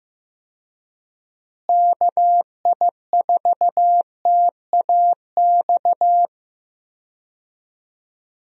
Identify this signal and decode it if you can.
Morse code, heard as a keyed tone, about 15 wpm, 700 Hz. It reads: KI4TAX
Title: USA Call Signs - 6 Characters - Repeated 3x - 15wpm